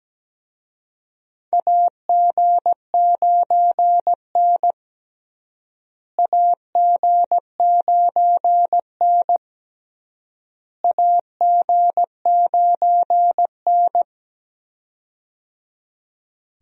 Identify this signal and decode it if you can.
Morse code, heard as a keyed tone, about 17 wpm, 700 Hz. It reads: AG9N AG9N AG9N
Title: USA Call Signs - 4 Characters - Repeated 3x - 17wpm